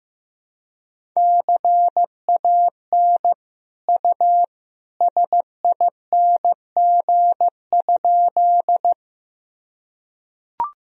Morse code, heard as a keyed tone, about 15 wpm, 700 Hz. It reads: CAN U SING?
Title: Sentences from Top 400 Words and Abbreviations 15wpm